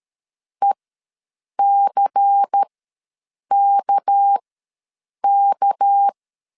Morse code, heard as a keyed tone, about 13 wpm, spaced as for 7 wpm, 800 Hz. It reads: ECKK
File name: LEZIONI / 6 PKX0 / 6 PK ws 12wpm 3116